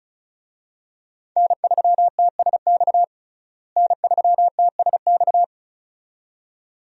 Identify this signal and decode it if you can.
Morse code, heard as a keyed tone, about 36 wpm, 700 Hz. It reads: N3TSX N3TSX
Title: USA Call Signs - 5 Characters - Repeated 2x - 35wpm